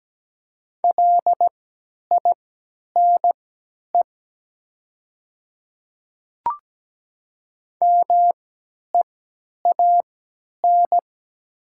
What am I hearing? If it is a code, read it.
Morse code, heard as a keyed tone, about 17 wpm, spaced as for 10 wpm, 700 Hz. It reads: LINE MEAN
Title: 3x Character Spacing - Top 200 Words - 17wpm